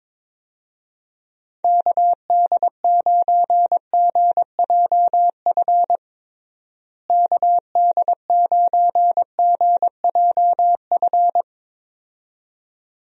Morse code, heard as a keyed tone, about 21 wpm, 700 Hz. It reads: KD9GJF KD9GJF
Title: USA Call Signs - 6 Characters - Repeated 2x - 22wpm